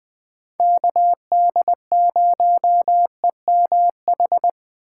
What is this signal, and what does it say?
Morse code, heard as a keyed tone, about 20 wpm, 700 Hz. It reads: KD0EMH
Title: USA Call Signs - 6 Characters - 2x Character Spacing - 20wpm